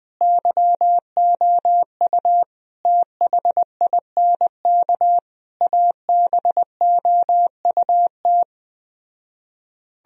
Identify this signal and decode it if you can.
Morse code, heard as a keyed tone, about 20 wpm, 700 Hz. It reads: YOU THINK ABOUT
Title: Speed-Racing - Sets of 5 Words 17wpm